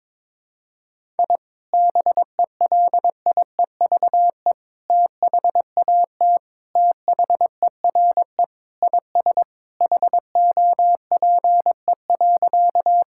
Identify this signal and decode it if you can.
Morse code, heard as a keyed tone, about 22 wpm, 700 Hz. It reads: I BELIEVE THAT THERE IS HOPE.
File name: practice / 22wpm / Sentences with 3-Word Phrases - 22wpm